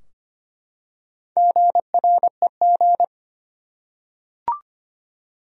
Morse code, heard as a keyed tone, about 25 wpm, 700 Hz. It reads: GREG
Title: USA Common Names - Max 4 Letters - 25wpm